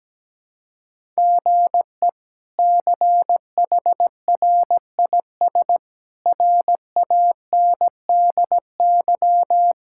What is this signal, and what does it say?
Morse code, heard as a keyed tone, about 17 wpm, 700 Hz. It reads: GE CHRIS RANDY
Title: K1USN SST QSO Practice - 17wpm